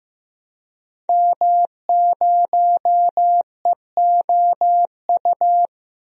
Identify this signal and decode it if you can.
Morse code, heard as a keyed tone, about 15 wpm, 700 Hz. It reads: M0EOU
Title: DXCC-100 Call Signs - 5 Characters - Repeated 3x - 15wpm